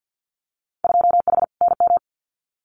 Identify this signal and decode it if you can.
Morse code, heard as a keyed tone, about 50 wpm, 700 Hz. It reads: 2H NM
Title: Winter Field Day Exchanges Encore 50wpm